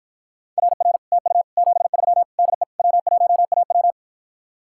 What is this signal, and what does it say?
Morse code, heard as a keyed tone, about 53 wpm, 700 Hz. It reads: RR TU 73 DE W1AW